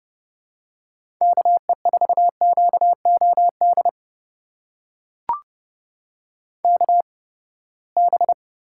Morse code, heard as a keyed tone, about 31 wpm, spaced as for 21 wpm, 700 Hz. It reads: KE4QOD KB
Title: USA Call Signs - 6 Characters - 8x Character Spacing - 30wpm